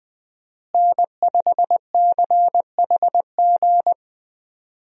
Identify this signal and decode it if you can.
Morse code, heard as a keyed tone, about 20 wpm, 700 Hz. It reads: N5CHG
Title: USA Call Signs - 5 Characters - Repeated 2x - 20wpm